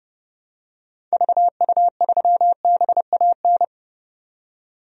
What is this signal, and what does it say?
Morse code, heard as a keyed tone, about 30 wpm, 700 Hz. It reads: VU3BAN
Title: DXCC-100 Call Signs - 6 Characters - 30wpm